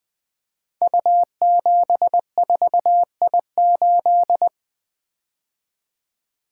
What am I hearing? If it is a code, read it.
Morse code, heard as a keyed tone, about 20 wpm, 700 Hz. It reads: U74I8
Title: Code Groups - 5 Letter:Numbers - 20wpm